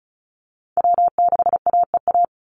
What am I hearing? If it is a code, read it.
Morse code, heard as a keyed tone, about 34 wpm, 700 Hz. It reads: W6AEA
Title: Call Signs Repeated 2x - 35wpm